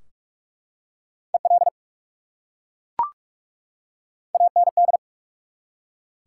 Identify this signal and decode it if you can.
Morse code, heard as a keyed tone, about 43 wpm, spaced as for 11 wpm, 700 Hz. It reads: ELAND
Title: Top 180 QSO Elements 45wpm